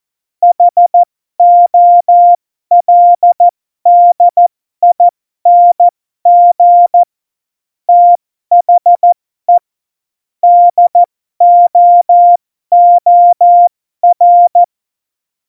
Morse code, heard as a keyed tone, about 14 wpm, spaced as for 12 wpm, 700 Hz. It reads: HOLDING THE DOOR
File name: Morsekurs / Witze